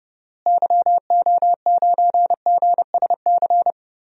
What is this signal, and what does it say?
Morse code, heard as a keyed tone, about 32 wpm, 700 Hz. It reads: YO9GSC
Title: DXCC-100 Call Signs - 6 Characters - Repeated 3x - 30wpm